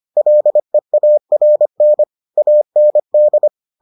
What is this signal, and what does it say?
Morse code, heard as a keyed tone, about 25 wpm, 600 Hz. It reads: LEARN AND